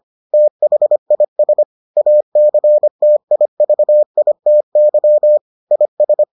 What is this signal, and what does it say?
Morse code, heard as a keyed tone, about 25 wpm, 600 Hz. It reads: THIS ACTIVITY IS